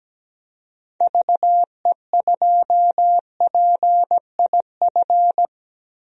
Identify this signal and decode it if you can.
Morse code, heard as a keyed tone, about 17 wpm, 700 Hz. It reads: VE2PIF
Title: DXCC-100 Call Signs - 6 Characters - Repeated 2x - 17wpm